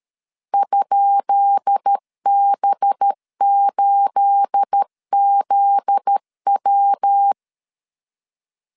Morse code, heard as a keyed tone, about 13 wpm, 800 Hz. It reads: ?B8ZW
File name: LEZIONI / 10 SIGNS / 10B INT 12wpm 2414